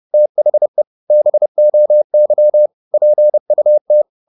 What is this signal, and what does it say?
Morse code, heard as a keyed tone, about 30 wpm, 600 Hz. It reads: THE BOY PUT
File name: the old man and the sea / the old man and the sea 1 30wpm